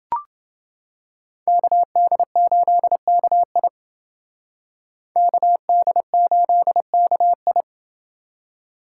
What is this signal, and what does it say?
Morse code, heard as a keyed tone, about 28 wpm, 700 Hz. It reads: KD8KI KD8KI